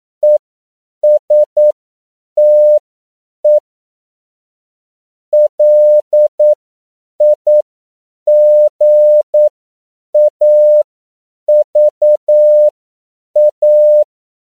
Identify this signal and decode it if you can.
Morse code, heard as a keyed tone, about 9 wpm, spaced as for 7 wpm, 600 Hz. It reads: ESTE LIGAVA
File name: CW / 10PPM